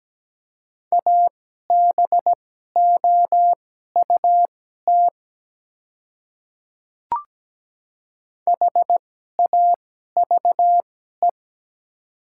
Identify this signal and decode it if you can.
Morse code, heard as a keyed tone, about 17 wpm, spaced as for 12 wpm, 700 Hz. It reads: ABOUT HAVE
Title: 2x Character Spacing - Top 200 Words - 17wpm